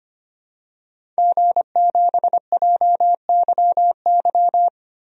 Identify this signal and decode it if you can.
Morse code, heard as a keyed tone, about 25 wpm, 700 Hz. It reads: G7JYY
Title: DXCC-100 Call Signs - 5 Characters - Repeated 2x - 25wpm